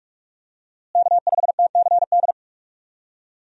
Morse code, heard as a keyed tone, about 46 wpm, 700 Hz. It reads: K5TCD
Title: USA Call Signs - 5 Characters - Repeated 2x - 45wpm